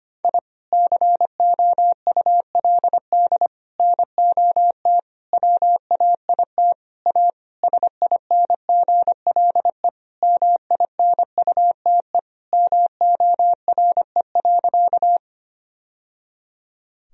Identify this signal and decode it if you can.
Morse code, heard as a keyed tone, about 25 wpm, 700 Hz. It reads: I COULD NOT WAIT A SINGLE MINUTE MORE.
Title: Sentences from Top 1000 Words - Review Part 4 - 25wpm